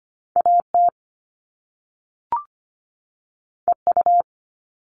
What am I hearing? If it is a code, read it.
Morse code, heard as a keyed tone, about 25 wpm, 700 Hz. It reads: A T E U
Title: All Cut Numbers - 2-Digit 25wpm